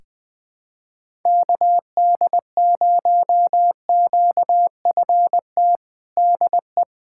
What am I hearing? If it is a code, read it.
Morse code, heard as a keyed tone, about 20 wpm, 700 Hz. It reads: KD0QFT DE
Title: USA Callsign DE Callsign - 6 Characters - 20wpm